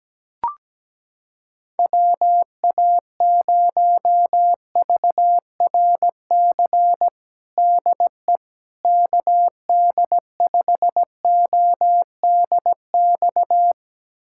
Morse code, heard as a keyed tone, about 17 wpm, 700 Hz. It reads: WA0VRC DE KD5ODX
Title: USA Callsign DE Callsign - 6 Characters - 17wpm